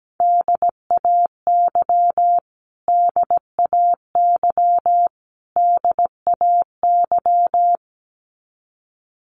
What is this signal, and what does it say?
Morse code, heard as a keyed tone, about 17 wpm, 700 Hz. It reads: DAY DAY DAY
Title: Top 200 Words Repeated 17wpm